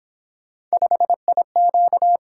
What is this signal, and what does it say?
Morse code, heard as a keyed tone, about 26 wpm, 700 Hz. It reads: 5IQ
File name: Speed-Racing - Three Letter-Number Combinations - 22wpm